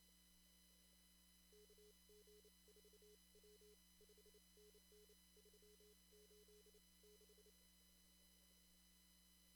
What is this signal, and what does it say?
Morse code, heard as a keyed tone, about 28 wpm, spaced as for 24 wpm, 420 Hz. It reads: KG4W 5NN 386
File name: CW / kt5j wpx cw-op-k5pi